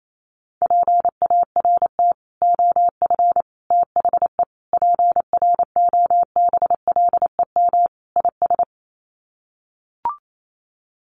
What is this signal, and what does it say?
Morse code, heard as a keyed tone, about 28 wpm, 700 Hz. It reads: PART OF THE PROBLEM IS
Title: Sets of 5 Words Encore2 28wpm